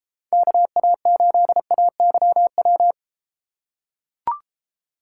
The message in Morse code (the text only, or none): KA8AYW